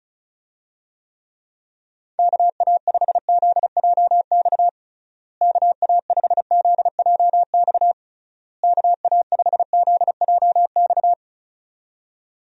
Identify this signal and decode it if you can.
Morse code, heard as a keyed tone, about 34 wpm, 700 Hz. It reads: KA5ZJX KA5ZJX KA5ZJX
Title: USA Call Signs - 6 Characters - Repeated 3x - 35wpm